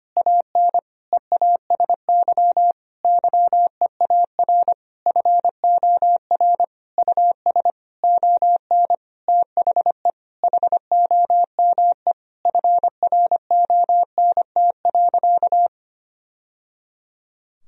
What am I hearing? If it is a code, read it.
Morse code, heard as a keyed tone, about 25 wpm, 700 Hz. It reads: AN EASY YEAR FOR US ON THE HOME FRONT.
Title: Sentences from Top 1000 Words - Review Part 2 - 25wpm